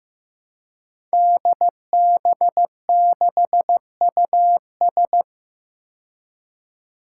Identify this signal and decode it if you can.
Morse code, heard as a keyed tone, about 15 wpm, 700 Hz. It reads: DB6US
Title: Five Challenging Letter-Number Combinations 15wpm